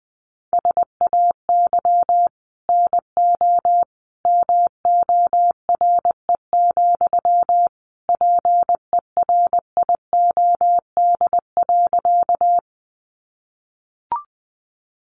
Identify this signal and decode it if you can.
Morse code, heard as a keyed tone, about 20 wpm, 700 Hz. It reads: SAY NO MORE, PERIOD.
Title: Sentences from Top 800 Words 20wpm